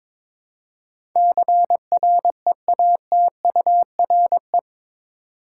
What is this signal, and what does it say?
Morse code, heard as a keyed tone, about 22 wpm, 700 Hz. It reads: CREATURE
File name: practice / 22wpm / Eight Letter Words - 22wpm